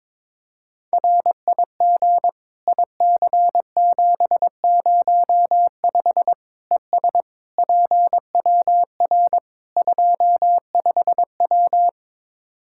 Morse code, heard as a keyed tone, about 22 wpm, 700 Hz. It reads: RIG IC705 ES PWR 25W
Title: Practice QSO - Exchange 2 - Info - 22wpm